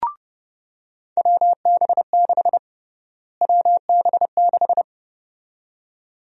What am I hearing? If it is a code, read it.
Morse code, heard as a keyed tone, about 30 wpm, 700 Hz. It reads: WB6 WB6
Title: USA Call Sign Prefixes - Repeated 2x - 30wpm